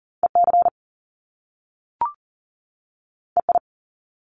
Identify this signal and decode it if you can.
Morse code, heard as a keyed tone, about 38 wpm, 700 Hz. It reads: EC EI
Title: Top Two Letter Combinations in Words 40wpm